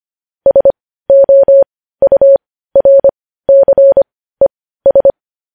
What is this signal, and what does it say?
Morse code, heard as a keyed tone, about 25 wpm, spaced as for 15 wpm, 550 Hz. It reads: SOURCES
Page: news-headlines-in-morse-code-at-15-wpm